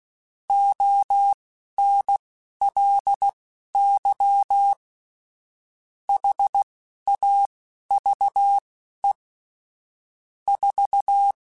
Morse code, heard as a keyed tone, about 16 wpm, spaced as for 11 wpm, 800 Hz. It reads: ONLY HAVE 4
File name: Technical Subjects / Technical Subjects-mp3s / AND Gate @10